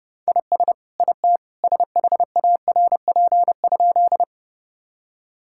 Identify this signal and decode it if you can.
Morse code, heard as a keyed tone, about 30 wpm, 700 Hz. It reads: IS IT SHARP?
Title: Sentences from Top 1600 Words 30wpm